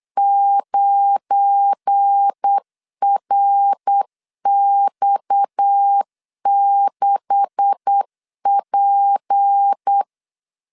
Morse code, heard as a keyed tone, about 8 wpm, 800 Hz. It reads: 9RX6P
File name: LEZIONI / 10 SIGNS / 10D COMA 8wpm 3749